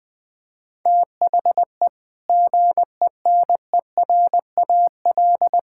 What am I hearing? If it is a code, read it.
Morse code, heard as a keyed tone, about 20 wpm, 700 Hz. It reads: THE GENERAL
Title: Sentences from Top 1000 Words - Review Part 5 - 20wpm